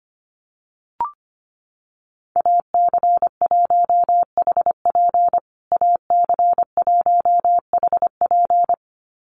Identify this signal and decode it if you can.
Morse code, heard as a keyed tone, about 25 wpm, 700 Hz. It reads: AC1HP AC1HP
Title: USA Call Signs Repeated 2x 25wpm